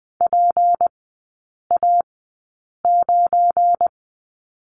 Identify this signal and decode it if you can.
Morse code, heard as a keyed tone, about 20 wpm, spaced as for 8 wpm, 690 Hz. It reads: PA9
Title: 172 Lesson 28e - Number 9 - Three Characters 20wpm